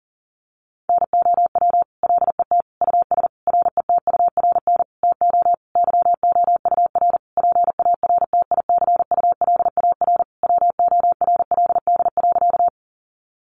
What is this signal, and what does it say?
Morse code, heard as a keyed tone, about 40 wpm, 700 Hz. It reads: NOW LET US RETURN TO YOUR PARTICULAR WORLD.
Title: Sentences from Top 800 Words 40wpm